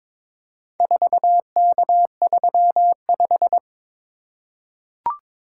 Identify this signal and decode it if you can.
Morse code, heard as a keyed tone, about 22 wpm, 700 Hz. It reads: 4K35
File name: practice / 22wpm / Four Letter-Number Combinations - 22wpm